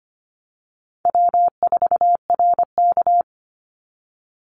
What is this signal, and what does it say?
Morse code, heard as a keyed tone, about 25 wpm, 700 Hz. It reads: W4RK
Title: NA SOTA Chaser Callsigns 25wpm